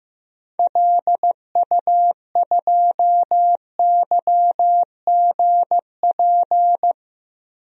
Morse code, heard as a keyed tone, about 15 wpm, 700 Hz. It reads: LU2YGP
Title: DXCC-100 Call Signs - 6 Characters - 15wpm